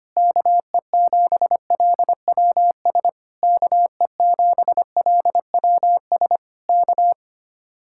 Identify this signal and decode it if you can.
Morse code, heard as a keyed tone, about 24 wpm, 700 Hz. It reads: KE7LWS KE7LWS K
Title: Speed-Racing Common QSO Phrases Encore - 25wpm